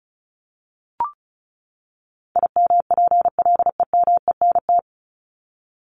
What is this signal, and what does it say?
Morse code, heard as a keyed tone, about 34 wpm, 700 Hz. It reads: IMPLEMENT
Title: Top 1501-1750 Words 35wpm